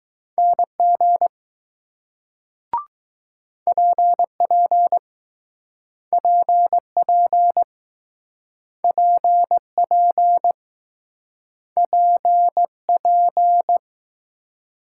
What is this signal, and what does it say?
Morse code, heard as a keyed tone, about 20 wpm, 700 Hz. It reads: NG PP PP PP P P